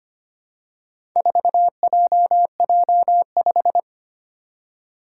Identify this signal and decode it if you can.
Morse code, heard as a keyed tone, about 25 wpm, 700 Hz. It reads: 4JJ5
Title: Four Challenging Letter-Number Combinations 25wpm